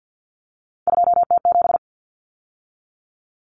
Morse code, heard as a keyed tone, about 50 wpm, 700 Hz. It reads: 2T7
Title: Common Cut Numbers - 3-Digit 50wpm